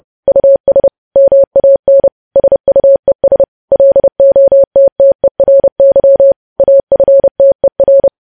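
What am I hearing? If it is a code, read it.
Morse code, heard as a keyed tone, about 30 wpm, 550 Hz. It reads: US MAN SUES LOTTERY AFTER
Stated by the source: News headlines sent in Morse at 30.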